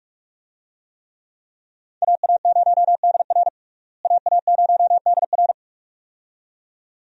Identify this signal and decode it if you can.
Morse code, heard as a keyed tone, about 41 wpm, 700 Hz. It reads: AA0DR AA0DR
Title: USA Call Signs - 5 Characters - Repeated 2x - 45wpm